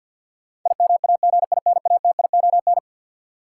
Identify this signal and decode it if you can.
Morse code, heard as a keyed tone, about 53 wpm, 700 Hz. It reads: IMAGINATION
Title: Sentences from Top 1300 Words - New Word Drill Alt - 50wpm